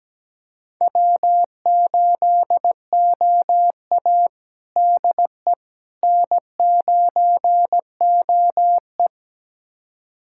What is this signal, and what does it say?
Morse code, heard as a keyed tone, about 17 wpm, 700 Hz. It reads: W8OA DE N9OE
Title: USA Callsign DE Callsign - 4 Characters - 17wpm